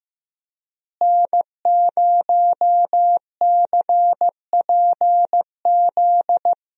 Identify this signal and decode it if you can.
Morse code, heard as a keyed tone, about 15 wpm, 700 Hz. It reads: N0CPZ
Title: USA Call Signs - 5 Characters - Repeated 3x - 15wpm